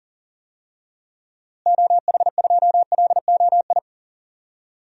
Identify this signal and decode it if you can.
Morse code, heard as a keyed tone, about 40 wpm, 700 Hz. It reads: OH2LOI
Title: DXCC-100 Call Signs - 6 Characters - Repeated 3x - 40wpm